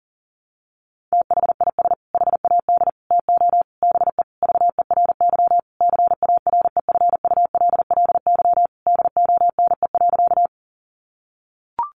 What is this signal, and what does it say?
Morse code, heard as a keyed tone, about 40 wpm, 700 Hz. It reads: THIS HAD TO BE VERY CAREFULLY DONE.
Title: Sentences from Top 1300 Words 40wpm